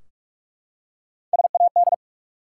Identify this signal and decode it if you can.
Morse code, heard as a keyed tone, about 46 wpm, 700 Hz. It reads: SAD
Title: Three Letter Words - 45wpm